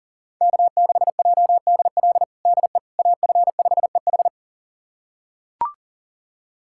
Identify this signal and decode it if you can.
Morse code, heard as a keyed tone, about 37 wpm, 700 Hz. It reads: K6JDL DE AF5EH
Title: USA Callsign DE Callsign - 5 Characters - 40wpm